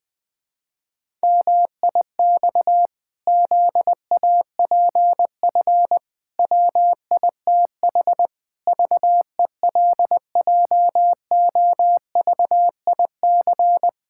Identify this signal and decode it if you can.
Morse code, heard as a keyed tone, about 21 wpm, 700 Hz. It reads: MIX ZAPF WITH VELJOVIC
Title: Pangram Challenge v2 - 20wpm